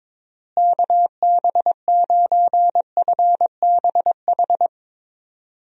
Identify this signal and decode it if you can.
Morse code, heard as a keyed tone, about 22 wpm, 700 Hz. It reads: KB9FBH
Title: USA Call Signs - 6 Characters - Repeated 2x - 22wpm